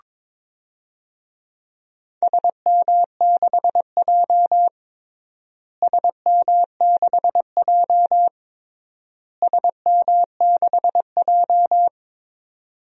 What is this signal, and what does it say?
Morse code, heard as a keyed tone, about 22 wpm, 700 Hz. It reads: SM6J SM6J SM6J
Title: DXCC-100 Call Signs - 4 Characters - Repeated 3x - 22wpm